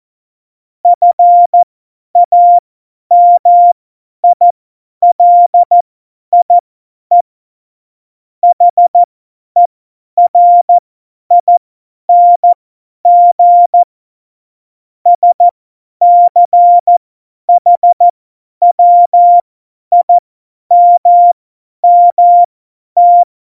Morse code, heard as a keyed tone, about 14 wpm, spaced as for 10 wpm, 700 Hz. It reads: FAMILIE HERING SCHWIMMT
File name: Morsekurs / Witze / Witz 03.Juli 2018, 50 F70